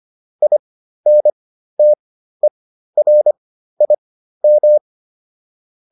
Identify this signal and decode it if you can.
Morse code, heard as a keyed tone, about 25 wpm, spaced as for 13 wpm, 600 Hz. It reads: INTERIM
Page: intermediate-practice-files